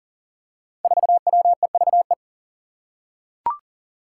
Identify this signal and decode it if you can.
Morse code, heard as a keyed tone, about 40 wpm, 700 Hz. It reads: 4WEVE
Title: Five Challenging Letter-Number Combinations 40wpm